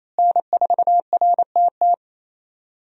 Morse code, heard as a keyed tone, about 28 wpm, 700 Hz. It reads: N4RTT
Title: USA Call Signs - 5 Characters - Repeated 2x - 28wpm